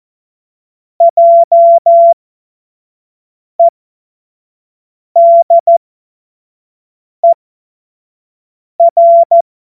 Morse code, heard as a keyed tone, about 14 wpm, spaced as for 5 wpm, 680 Hz. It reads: JEDER
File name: Witz-07-Oktober-2025-25-F70